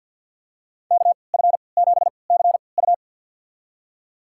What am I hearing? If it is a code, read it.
Morse code, heard as a keyed tone, about 50 wpm, spaced as for 28 wpm, 700 Hz. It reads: KF7CU
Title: USA Call Signs - 5 Characters - 3x Character Spacing - 50wpm